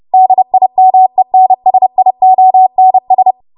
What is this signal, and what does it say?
Morse code, heard as a keyed tone, about 30 wpm, 750 Hz. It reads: DIMENSIONS
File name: Morse / 30WPM